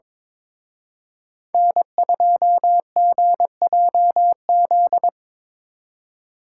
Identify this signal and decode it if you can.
Morse code, heard as a keyed tone, about 25 wpm, 700 Hz. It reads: N2GJZ
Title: Speed Racing - USA 1x3 Call Signs 22wpm